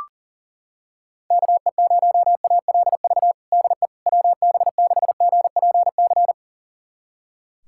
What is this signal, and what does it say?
Morse code, heard as a keyed tone, about 41 wpm, 690 Hz. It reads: KE0ALV DE WB6GPC
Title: USA Callsign DE Callsign - 6 Characters - 40wpm